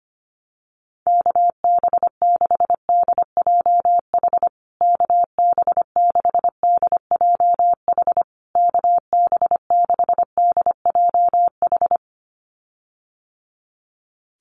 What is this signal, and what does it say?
Morse code, heard as a keyed tone, about 25 wpm, 700 Hz. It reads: KB6DJH KB6DJH KB6DJH
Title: USA Call Signs Repeated 3x 25wpm